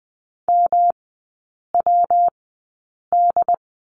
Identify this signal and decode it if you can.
Morse code, heard as a keyed tone, about 20 wpm, spaced as for 8 wpm, 690 Hz. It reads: MWD
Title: 191 Lesson 31e - Letter v - Three Characters 20wpm